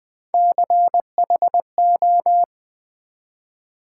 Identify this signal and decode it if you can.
Morse code, heard as a keyed tone, about 20 wpm, 700 Hz. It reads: CHO
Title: Build A Word - 3 Letters At A Time - 6-Letter Words - 20wpm